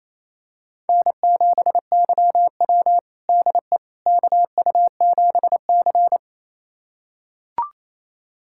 Morse code, heard as a keyed tone, about 27 wpm, 700 Hz. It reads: N7YW DE KU7C
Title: USA Callsign DE Callsign - 4 Characters - 28wpm